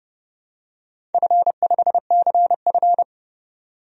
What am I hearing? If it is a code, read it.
Morse code, heard as a keyed tone, about 29 wpm, 700 Hz. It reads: F5CF
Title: DXCC-100 Call Signs - 4 Characters - 30wpm